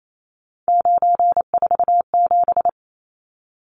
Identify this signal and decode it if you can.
Morse code, heard as a keyed tone, about 28 wpm, 700 Hz. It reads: 947
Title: Numbers - 3-Digit 28wpm